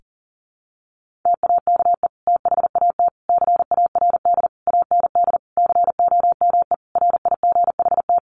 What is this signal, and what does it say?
Morse code, heard as a keyed tone, about 40 wpm, 700 Hz. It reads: TAKE THAT CARD AND COME RIGHT
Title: Sentences from Top 1400 Words 40wpm